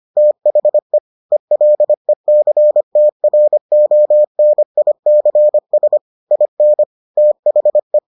Morse code, heard as a keyed tone, about 25 wpm, 600 Hz. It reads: THE ELECTRONICS IN THE